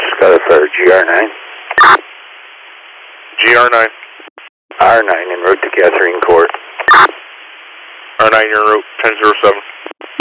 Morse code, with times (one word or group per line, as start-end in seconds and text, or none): none